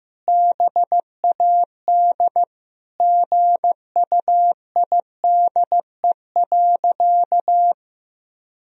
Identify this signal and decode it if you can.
Morse code, heard as a keyed tone, about 16 wpm, 700 Hz. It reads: BAD GUIDE.
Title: Sentences from Top 1000 Words - Review Part 3 - 15wpm